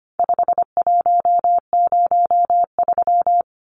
Morse code, heard as a keyed tone, about 25 wpm, 700 Hz. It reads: 5103
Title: Numbers - 4-Digit 25wpm